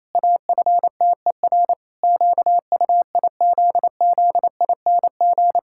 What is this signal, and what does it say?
Morse code, heard as a keyed tone, about 27 wpm, 700 Hz. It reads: AFTER QUIZZING
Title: Pangram Challenge v2 - 28wpm